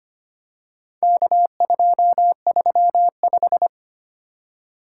Morse code, heard as a keyed tone, about 25 wpm, 700 Hz. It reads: K235
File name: practice / 25wpm / Four Letter-Number Combinations - 25wpm